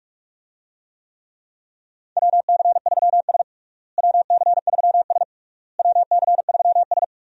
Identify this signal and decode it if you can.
Morse code, heard as a keyed tone, about 42 wpm, 700 Hz. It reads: WC3S WC3S WC3S
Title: USA Call Signs - 4 Characters - Repeated 3x - 45wpm